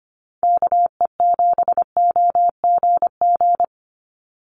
Morse code, heard as a keyed tone, about 25 wpm, 700 Hz. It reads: KE7OGG
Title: USA Call Signs 25wpm